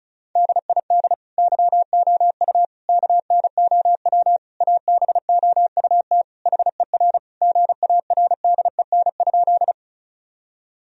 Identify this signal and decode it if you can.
Morse code, heard as a keyed tone, about 34 wpm, 700 Hz. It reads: DID YOU KNOW ABOUT HER GARDEN?
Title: Sentences from Top 1000 Words - Review Part 4 - 35wpm